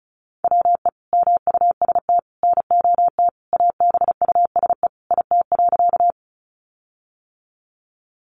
Morse code, heard as a keyed tone, about 34 wpm, 700 Hz. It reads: WE MUST NOT ABUSE IT.
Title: Sentences from Top 1300 Words 35wpm